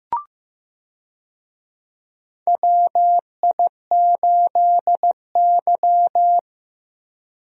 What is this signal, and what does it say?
Morse code, heard as a keyed tone, about 15 wpm, 700 Hz. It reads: WI8Y
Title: USA Call Signs - 4 Characters - Repeated 3x - 15wpm